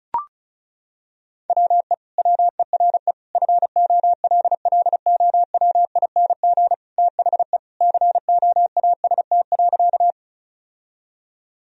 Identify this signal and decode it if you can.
Morse code, heard as a keyed tone, about 36 wpm, 700 Hz. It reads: WE WERE FOLLOWING THE COAST.
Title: Sentences from Top 1000 Words - Review Part 4 - 35wpm